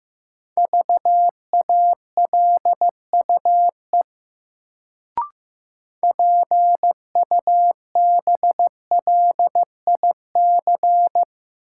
Morse code, heard as a keyed tone, about 15 wpm, 700 Hz. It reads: VALUE PUBLIC
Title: Sentences from Top 600 Words - New Word Drill - 15wpm